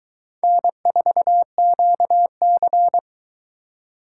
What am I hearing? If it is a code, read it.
Morse code, heard as a keyed tone, about 21 wpm, 700 Hz. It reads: N4QC